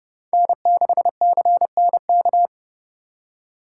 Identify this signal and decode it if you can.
Morse code, heard as a keyed tone, about 36 wpm, 700 Hz. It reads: N6CNK
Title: Speed Racing - USA 1x3 Call Signs 30wpm